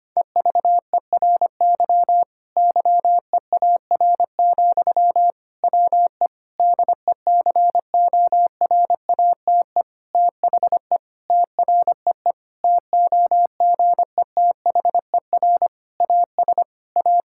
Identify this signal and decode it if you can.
Morse code, heard as a keyed tone, about 25 wpm, 700 Hz. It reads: EVERY YEAR, WE DECORATE THE TREE TOGETHER AS A
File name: practice / 25wpm / Difficult Sentences with Hints - 25wpm